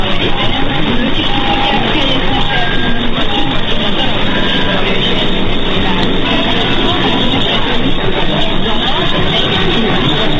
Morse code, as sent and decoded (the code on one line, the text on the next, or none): none